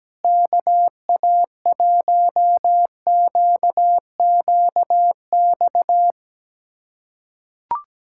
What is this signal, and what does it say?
Morse code, heard as a keyed tone, about 17 wpm, 700 Hz. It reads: KA1QQX
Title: USA Callsign DE Callsign - 6 Characters - 17wpm